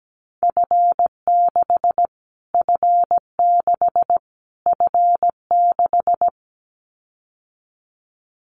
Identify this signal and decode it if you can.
Morse code, heard as a keyed tone, about 17 wpm, 700 Hz. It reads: F6 F6 F6
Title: Callsign Prefixes for Top 25 DXCC - Repeated 3x - 17wpm